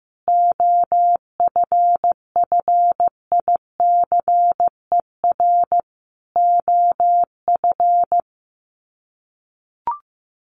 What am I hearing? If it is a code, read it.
Morse code, heard as a keyed tone, about 15 wpm, 700 Hz. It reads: OFFICER OF
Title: Sets of 5 Words Encore2 15wpm